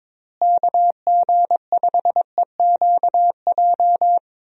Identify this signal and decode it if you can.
Morse code, heard as a keyed tone, about 21 wpm, 700 Hz. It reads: KG5EQJ
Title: USA Call Signs - 6 Characters - Repeated 3x - 22wpm